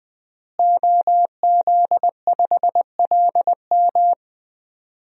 Morse code, heard as a keyed tone, about 20 wpm, 700 Hz. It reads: OZ5LM
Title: DXCC-100 Call Signs - 5 Characters - Repeated 3x - 20wpm